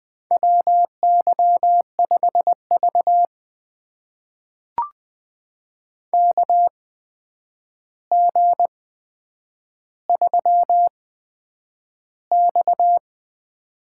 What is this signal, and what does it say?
Morse code, heard as a keyed tone, about 20 wpm, spaced as for 9 wpm, 700 Hz. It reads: WY5V KG3X
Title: USA Call Signs - 4 Characters - 8x Character Spacing - 20wpm